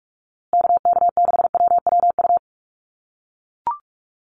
Morse code, heard as a keyed tone, about 44 wpm, 700 Hz. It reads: KK6WWU
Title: USA Call Signs Repeated 3x 45wpm